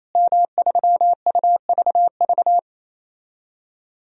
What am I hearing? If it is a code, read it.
Morse code, heard as a keyed tone, about 27 wpm, 700 Hz. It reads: M3UVV
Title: DXCC-100 Call Signs - 5 Characters - Repeated 3x - 28wpm